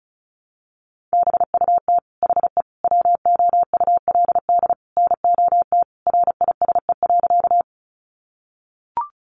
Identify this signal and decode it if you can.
Morse code, heard as a keyed tone, about 34 wpm, 700 Hz. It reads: BUT HE WOULD NOT RISE.
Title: Sentences from Top 900 Words 35wpm